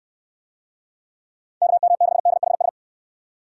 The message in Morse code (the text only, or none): KM6DHS